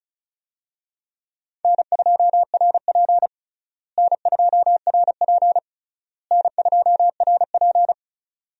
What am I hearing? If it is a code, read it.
Morse code, heard as a keyed tone, about 36 wpm, 700 Hz. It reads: N2RP N2RP N2RP
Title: USA Call Signs - 4 Characters - Repeated 3x - 35wpm